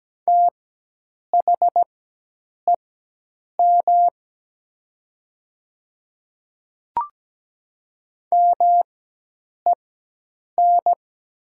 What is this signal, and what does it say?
Morse code, heard as a keyed tone, about 17 wpm, spaced as for 8 wpm, 700 Hz. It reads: THEM MEN
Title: 4x Character Spacing - Top 200 Words - 17wpm